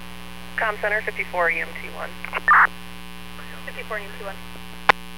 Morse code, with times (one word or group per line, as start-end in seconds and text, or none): none